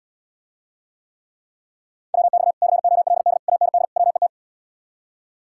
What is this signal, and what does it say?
Morse code, heard as a keyed tone, about 78 wpm, 700 Hz. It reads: WX COLD AND WET